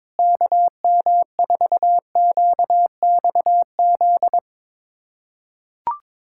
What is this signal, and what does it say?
Morse code, heard as a keyed tone, about 22 wpm, 700 Hz. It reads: KM4QXZ
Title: USA Callsign DE Callsign - 6 Characters - 22wpm